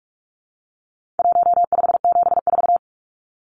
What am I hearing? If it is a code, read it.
Morse code, heard as a keyed tone, about 44 wpm, 700 Hz. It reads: 1574
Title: Numbers - 4-Digit 45wpm